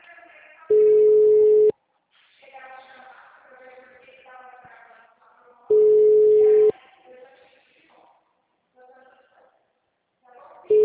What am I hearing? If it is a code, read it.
Ringback tone (Europe, 425 Hz), with speech.
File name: out- -2006- - - .40261